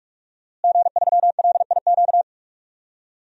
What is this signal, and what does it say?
Morse code, heard as a keyed tone, about 46 wpm, 700 Hz. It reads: G3LIQ
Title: DXCC-100 Call Signs - 5 Characters - Repeated 3x - 45wpm